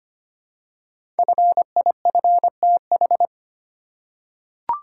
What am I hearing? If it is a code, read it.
Morse code, heard as a keyed tone, about 25 wpm, 700 Hz. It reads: FIFTH
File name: 5 Letter Words - First 3 Letters Given - 25wpm